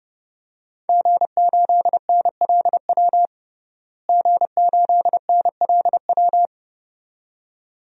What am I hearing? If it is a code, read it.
Morse code, heard as a keyed tone, about 30 wpm, 700 Hz. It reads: G8NLW G8NLW
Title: DXCC-100 Call Signs - 5 Characters - Repeated 2x - 30wpm